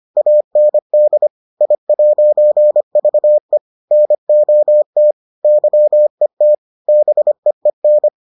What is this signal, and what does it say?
Morse code, heard as a keyed tone, about 25 wpm, 600 Hz. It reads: AND I'VE NOT YET BEEN